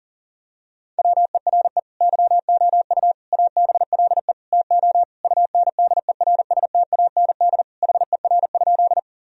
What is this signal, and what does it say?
Morse code, heard as a keyed tone, about 37 wpm, 700 Hz. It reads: WERE YOU ABLE TO UNDERSTAND HER?
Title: Sentences from Top 1000 Words - Review Part 2 - 40wpm